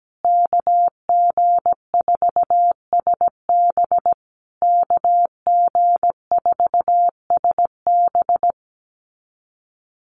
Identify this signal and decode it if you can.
Morse code, heard as a keyed tone, about 17 wpm, 700 Hz. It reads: KG4SB KG4SB
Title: USA Call Signs Repeated 3x 17wpm